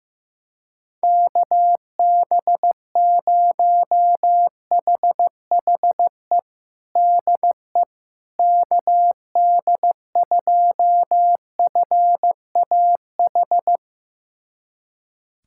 Morse code, heard as a keyed tone, about 16 wpm, 700 Hz. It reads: KB0HHE DE KD2FAH
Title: USA Callsign DE Callsign 15wpm